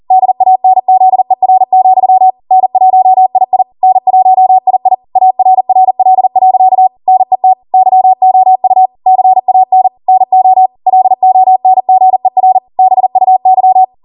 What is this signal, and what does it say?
Morse code, heard as a keyed tone, about 40 wpm, 750 Hz. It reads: DANZER, N1II N1II ARRL.NET YOU CAN NO LONGER BUY